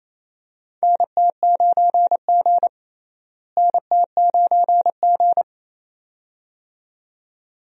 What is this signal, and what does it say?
Morse code, heard as a keyed tone, about 28 wpm, 700 Hz. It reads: NT9G NT9G
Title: USA Call Signs - 4 Characters - Repeated 2x - 28wpm